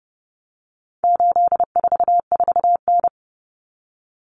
Morse code, heard as a keyed tone, about 30 wpm, 700 Hz. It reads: 844N
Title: Common Cut Numbers - 4-Digit 30wpm